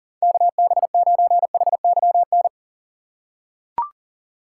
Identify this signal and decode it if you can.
Morse code, heard as a keyed tone, about 40 wpm, 700 Hz. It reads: KB9HYN
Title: USA Call Signs - 6 Characters - 2x Character Spacing - 40wpm